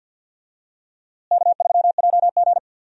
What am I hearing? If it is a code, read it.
Morse code, heard as a keyed tone, about 52 wpm, 700 Hz. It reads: K3JG
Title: USA Call Signs - 4 Characters - Repeated 3x - 50wpm